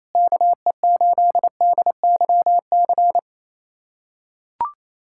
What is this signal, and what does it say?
Morse code, heard as a keyed tone, about 28 wpm, 700 Hz. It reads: KE8DYC
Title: USA Callsign DE Callsign - 6 Characters - 28wpm